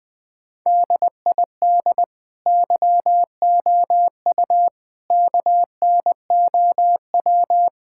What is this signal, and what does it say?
Morse code, heard as a keyed tone, about 20 wpm, 700 Hz. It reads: DID YOU KNOW
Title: Sentences with 9-Word Phrases - 20wpm